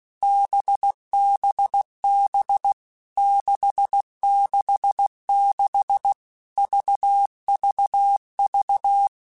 Morse code, heard as a keyed tone, about 16 wpm, 800 Hz. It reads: BBB 666 VVV
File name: Basic Letter Groups By7Groups / VB46 / VB46@15